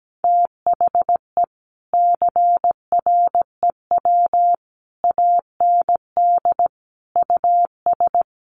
Morse code, heard as a keyed tone, about 17 wpm, 700 Hz. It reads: THE CREW AND US
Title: Sentences from Top 1500 Words 17wpm